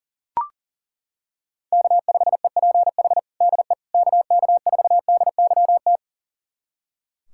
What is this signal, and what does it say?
Morse code, heard as a keyed tone, about 40 wpm, 700 Hz. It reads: K5EPH DE KK4DYT
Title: USA Callsign DE Callsign 40wpm